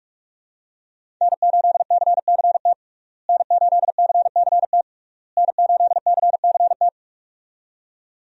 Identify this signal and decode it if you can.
Morse code, heard as a keyed tone, about 46 wpm, 700 Hz. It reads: N8CCT N8CCT N8CCT
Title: USA Call Signs - 5 Characters - Repeated 3x - 45wpm